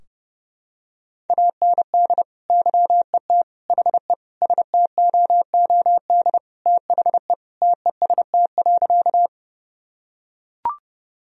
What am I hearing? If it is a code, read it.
Morse code, heard as a keyed tone, about 30 wpm, 700 Hz. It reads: AND YET HE STOOD THE TEST.
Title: Sentences from Top 1000 Words - Review Part 3 - 30wpm